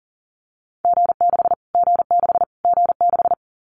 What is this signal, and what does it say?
Morse code, heard as a keyed tone, about 40 wpm, 700 Hz. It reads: G6 G6 G6
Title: Callsign Prefixes for Top 25 DXCC - Repeated 3x - 40wpm